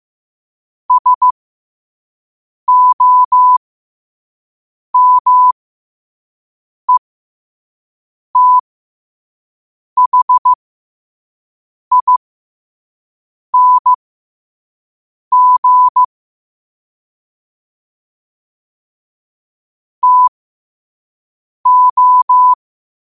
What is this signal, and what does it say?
Morse code, heard as a keyed tone, about 15 wpm, spaced as for 5 wpm, 1000 Hz. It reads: SOMETHING TO
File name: qotdfiles / 5wpm